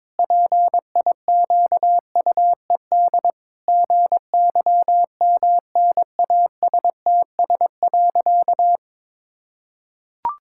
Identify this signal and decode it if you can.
Morse code, heard as a keyed tone, about 22 wpm, 700 Hz. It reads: PIQUED GYMNASTS.
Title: Pangram Challenge v2 - 22wpm